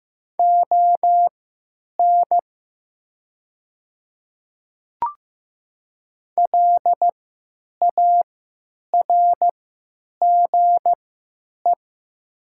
Morse code, heard as a keyed tone, about 15 wpm, spaced as for 9 wpm, 700 Hz. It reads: ON LARGE
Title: 3x Character Spacing - Top 200 Words - 15wpm